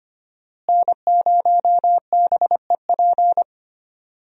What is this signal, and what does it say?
Morse code, heard as a keyed tone, about 25 wpm, 700 Hz. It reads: N0BEP
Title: USA Call Signs - 5 Characters - Repeated 3x - 25wpm